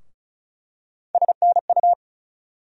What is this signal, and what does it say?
Morse code, heard as a keyed tone, about 36 wpm, 700 Hz. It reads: SNU
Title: USA Call Sign Postfixes - 3 Letters 35wpm